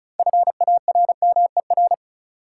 Morse code, heard as a keyed tone, about 34 wpm, 700 Hz. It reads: FARMER
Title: Six Letter Words - 35wpm